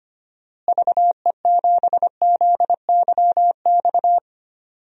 Morse code, heard as a keyed tone, about 25 wpm, 700 Hz. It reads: VE7ZYX
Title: DXCC-100 Call Signs - 6 Characters - Repeated 2x - 25wpm